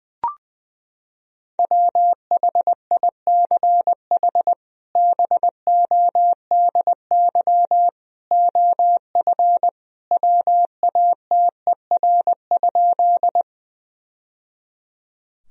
Morse code, heard as a keyed tone, about 21 wpm, 700 Hz. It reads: WHICH BODY OF WATER?
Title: Sentences from Top 1000 Words - Review Part 2 - 20wpm